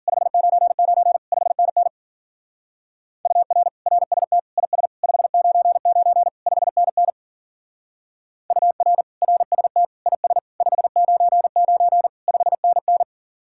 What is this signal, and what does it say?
Morse code, heard as a keyed tone, about 48 wpm, 700 Hz. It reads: 599 5NN UR RST IS 599 5NN UR RST IS 599 5NN
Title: Speed-Racing Common QSO Phrases - 40wpm